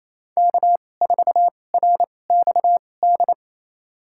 Morse code, heard as a keyed tone, about 28 wpm, spaced as for 20 wpm, 700 Hz. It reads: K4RXD
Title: USA Call Signs - 5 Characters - 2x Character Spacing - 28wpm